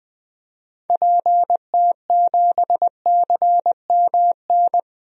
Morse code, heard as a keyed tone, about 20 wpm, 700 Hz. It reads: PT7CMN
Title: DXCC-100 Call Signs - 6 Characters - Repeated 2x - 20wpm